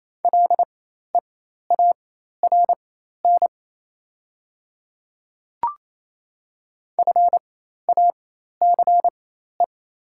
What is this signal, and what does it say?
Morse code, heard as a keyed tone, about 28 wpm, spaced as for 13 wpm, 700 Hz. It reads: LEARN FACE
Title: 4x Character Spacing - Top 300 Words - 28wpm